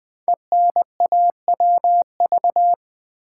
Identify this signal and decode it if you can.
Morse code, heard as a keyed tone, about 20 wpm, 700 Hz. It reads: ENAWV
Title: Code Groups - 5 Letters - 20wpm